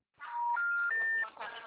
Special information tone, with noise.
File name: out- -2007- - - .20518